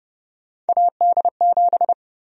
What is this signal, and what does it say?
Morse code, heard as a keyed tone, about 30 wpm, 700 Hz. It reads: AD7
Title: USA Call Sign Prefixes - Repeated 2x - 30wpm